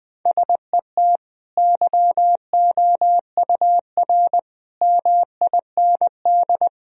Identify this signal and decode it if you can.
Morse code, heard as a keyed tone, about 19 wpm, 700 Hz. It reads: SET YOUR MIND
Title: Sentences from Top 1000 Words - Review Part 1 - 20wpm